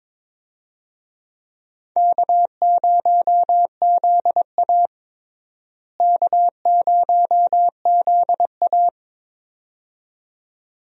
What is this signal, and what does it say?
Morse code, heard as a keyed tone, about 21 wpm, 700 Hz. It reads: K0ZA K0ZA
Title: USA Call Signs - 4 Characters - Repeated 2x - 22wpm